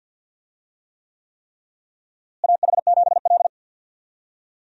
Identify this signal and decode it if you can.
Morse code, heard as a keyed tone, about 47 wpm, 700 Hz. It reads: AH7L
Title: USA Call Signs - 4 Characters - Repeated 3x - 50wpm